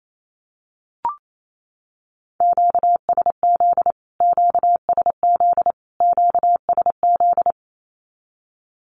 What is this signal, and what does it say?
Morse code, heard as a keyed tone, about 28 wpm, 700 Hz. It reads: QSZ QSZ QSZ
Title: Q-Codes Repeated 3x 28wpm